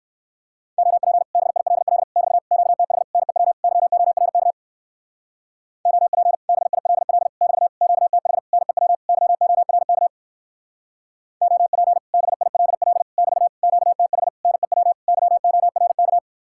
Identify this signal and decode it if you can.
Morse code, heard as a keyed tone, about 56 wpm, 700 Hz. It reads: OP BILL <BT> QTH NEW YORK OP BILL <BT> QTH NEW YORK OP BILL <BT> QTH NEW YORK
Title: Speed-Racing Common QSO Phrases Encore - 45wpm